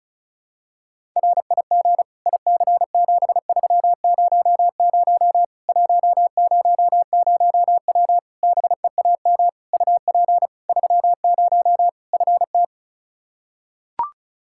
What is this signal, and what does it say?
Morse code, heard as a keyed tone, about 34 wpm, 700 Hz. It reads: RIG IC7300 100W BEAM UP 30 FT
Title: Practice QSO - Exchange 2 - Info - 35wpm